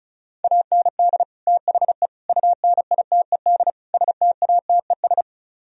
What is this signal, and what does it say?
Morse code, heard as a keyed tone, about 44 wpm, 700 Hz. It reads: AND THE UNITED STATES
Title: Speed-Racing - Sets of 4 Words 35wpm